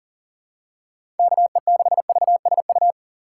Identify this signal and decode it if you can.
Morse code, heard as a keyed tone, about 37 wpm, 700 Hz. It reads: KE6VSU
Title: USA Call Signs - 4x Character Spacing - 40wpm